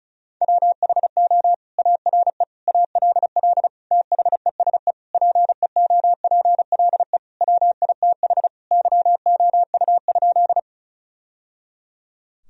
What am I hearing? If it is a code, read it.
Morse code, heard as a keyed tone, about 34 wpm, 700 Hz. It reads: WHO ARE ALL THESE PEOPLE WITH YOU?
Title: Sentences from Top 1000 Words - Review Part 1 - 35wpm